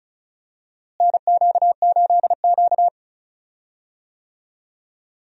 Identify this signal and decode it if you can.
Morse code, heard as a keyed tone, about 36 wpm, 700 Hz. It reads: NQ8Q
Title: USA Call Signs - 4 Characters - Repeated 2x - 35wpm